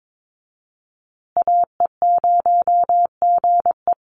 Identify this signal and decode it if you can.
Morse code, heard as a keyed tone, about 22 wpm, 700 Hz. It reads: AE0GE
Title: USA Call Signs Repeated 2x 22wpm